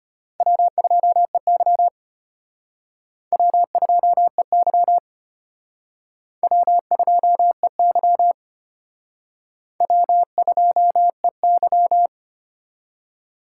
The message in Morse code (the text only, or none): W2EY W2EY W2EY W2EY